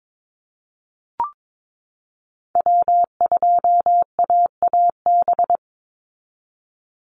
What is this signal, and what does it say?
Morse code, heard as a keyed tone, about 22 wpm, 700 Hz. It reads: W2AAB
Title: Call Signs Repeated 2x - 22wpm